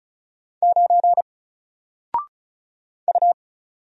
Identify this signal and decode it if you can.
Morse code, heard as a keyed tone, about 36 wpm, 700 Hz. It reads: 9 U
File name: Single-Letter-Number-35wpm